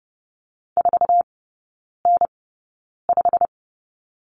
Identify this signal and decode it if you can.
Morse code, heard as a keyed tone, about 30 wpm, spaced as for 9 wpm, 700 Hz. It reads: 4N5
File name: course / 30:10wpm / 087 Lesson 15e - Number 5 - Three Characters 30wpm